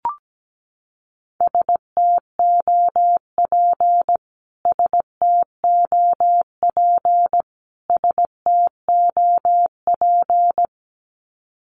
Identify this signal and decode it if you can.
Morse code, heard as a keyed tone, about 17 wpm, 700 Hz. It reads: STOP STOP STOP
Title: Top 300 Words Repeated 17wpm